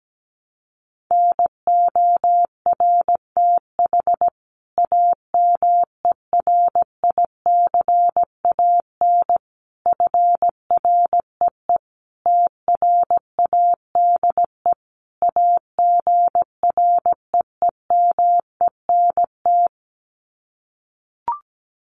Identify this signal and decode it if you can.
Morse code, heard as a keyed tone, about 17 wpm, 700 Hz. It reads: NORTH AMERICAN FREE TRADE AGREEMENT
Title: Sets of 5 Words Encore 17wpm